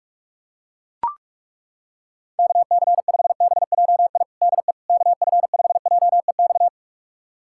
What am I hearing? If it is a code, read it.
Morse code, heard as a keyed tone, about 41 wpm, 700 Hz. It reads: KC5BJI DE KF5JEX
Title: USA Callsign DE Callsign - 6 Characters - 45wpm